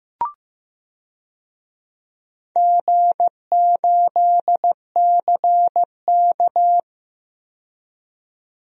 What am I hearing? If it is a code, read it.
Morse code, heard as a keyed tone, about 15 wpm, 700 Hz. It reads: G8CK
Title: DXCC-100 Call Signs - 4 Characters - 15wpm